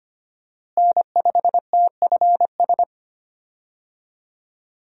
Morse code, heard as a keyed tone, about 25 wpm, 700 Hz. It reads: N5TFS
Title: USA Call Signs - 5 Characters - Repeated 2x - 25wpm